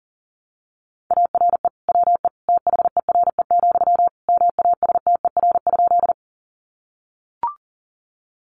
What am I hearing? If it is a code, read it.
Morse code, heard as a keyed tone, about 40 wpm, 700 Hz. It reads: ARE WE THERE, MASTER?
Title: Sentences from Top 1300 Words 40wpm